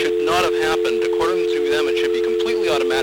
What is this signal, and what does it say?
Dial tone (North America), 350 + 440 Hz, with speech.
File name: blackratchet / recordings / Verizon411